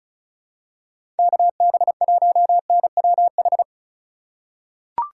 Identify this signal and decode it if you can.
Morse code, heard as a keyed tone, about 36 wpm, 700 Hz. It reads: KB1NWH
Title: USA Call Signs - 6 Characters - 3x Character Spacing - 35wpm